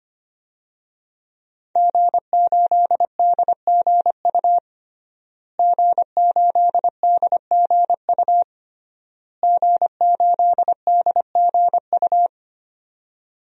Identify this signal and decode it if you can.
Morse code, heard as a keyed tone, about 25 wpm, 700 Hz. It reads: G8DGU G8DGU G8DGU
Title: DXCC-100 Call Signs - 5 Characters - Repeated 3x - 25wpm